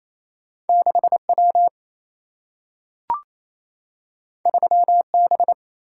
Morse code, heard as a keyed tone, about 28 wpm, 700 Hz. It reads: 6 W 3 B
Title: Single-Letter Single-Number 28wpm